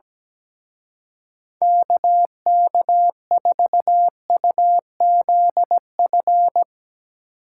Morse code, heard as a keyed tone, about 18 wpm, 700 Hz. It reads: KK4UZF
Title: Speed Racing - USA Call Signs 17wpm